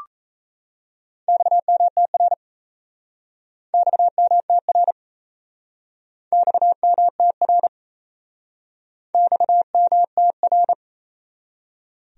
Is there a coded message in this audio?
Yes: Morse code from 1.3 to 10.7 s, heard as a keyed tone, about 37 wpm, 700 Hz. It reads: XMTR XMTR XMTR XMTR